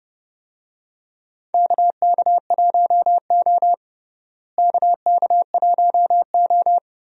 Morse code, heard as a keyed tone, about 30 wpm, 700 Hz. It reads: KK1O KK1O
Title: USA Call Signs - 4 Characters - Repeated 3x - 30wpm